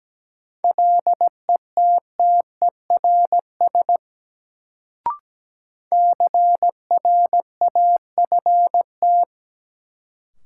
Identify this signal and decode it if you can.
Morse code, heard as a keyed tone, about 17 wpm, 700 Hz. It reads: LETTERS CRAFT
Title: Sentences from Top 1000 Words - Review Part 6 - New Word Drill - 17wpm